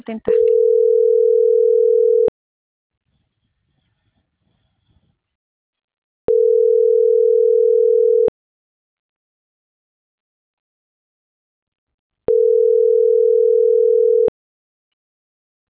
Ringback tone (North America), 440 + 480 Hz.
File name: CENTRAL MORENA / out jacque.aguilera.25